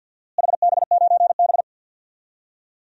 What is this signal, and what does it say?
Morse code, heard as a keyed tone, about 51 wpm, 700 Hz. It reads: HB9B
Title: DXCC-100 Call Signs - 4 Characters - Repeated 2x - 50wpm